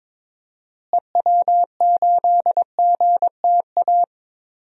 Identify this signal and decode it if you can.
Morse code, heard as a keyed tone, about 21 wpm, 700 Hz. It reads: EW8GTA
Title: DXCC-100 Call Signs - 6 Characters - Repeated 3x - 22wpm